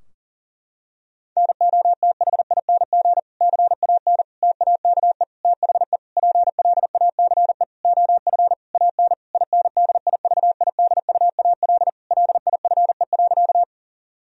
Morse code, heard as a keyed tone, about 40 wpm, 700 Hz. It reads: NOTHING CAN TAKE THE PLACE OF AN INDIVIDUAL LIFE.
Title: Sentences from Top 1000 Words - Review Part 4 - 40wpm